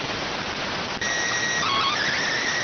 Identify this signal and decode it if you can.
SSTV (slow-scan television), PD 290.